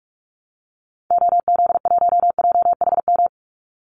Morse code, heard as a keyed tone, about 44 wpm, 700 Hz. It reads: OZ1JHM
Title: Call Signs Repeated 2x - 45wpm